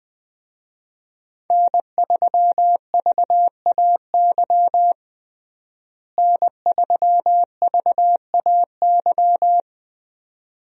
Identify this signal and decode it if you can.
Morse code, heard as a keyed tone, about 20 wpm, 700 Hz. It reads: N3VAY N3VAY
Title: USA Call Signs - 5 Characters - Repeated 2x - 20wpm